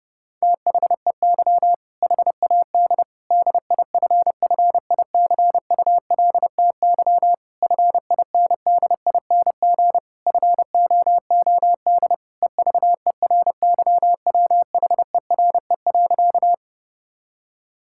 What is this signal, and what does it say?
Morse code, heard as a keyed tone, about 30 wpm, 700 Hz. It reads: THEY HAD DIFFICULTY FINDING FOOD EVERYWHERE.
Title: Sentences from Top 1600 Words 30wpm